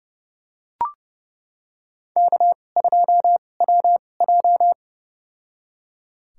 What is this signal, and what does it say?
Morse code, heard as a keyed tone, about 30 wpm, spaced as for 22 wpm, 700 Hz. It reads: K2WJ
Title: USA Call Signs - 4 Characters - 2x Character Spacing - 30wpm